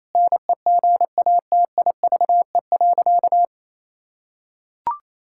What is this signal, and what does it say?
Morse code, heard as a keyed tone, about 28 wpm, 700 Hz. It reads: NEGATIVE.
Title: Sentences from Top 1000 Words - Review Part 6 - 28wpm